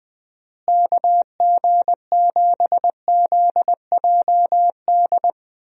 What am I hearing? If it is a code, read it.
Morse code, heard as a keyed tone, about 20 wpm, 700 Hz. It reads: KG7ZJD
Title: USA Call Signs - 6 Characters - Repeated 3x - 20wpm